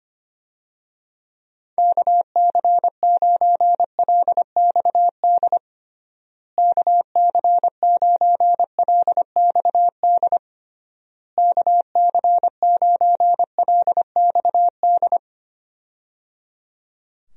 Morse code, heard as a keyed tone, about 25 wpm, 700 Hz. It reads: KC9LXD KC9LXD KC9LXD
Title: USA Call Signs - 6 Characters - Repeated 3x - 25wpm